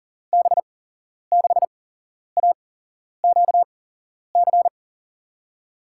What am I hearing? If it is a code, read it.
Morse code, heard as a keyed tone, about 40 wpm, spaced as for 11 wpm, 700 Hz. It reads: B6AQC
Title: USA Call Signs - 6 Characters - 8x Character Spacing - 40wpm